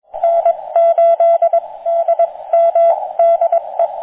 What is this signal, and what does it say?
Morse code, heard as a keyed tone, about 21 wpm, spaced as for 18 wpm, 680 Hz, with radio noise. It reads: N8DM DE